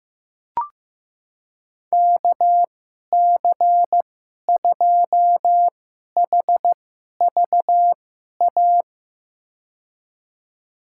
Morse code, heard as a keyed tone, about 15 wpm, spaced as for 11 wpm, 700 Hz. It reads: KC2HVA
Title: USA Call Signs - 6 Characters - 2x Character Spacing - 15wpm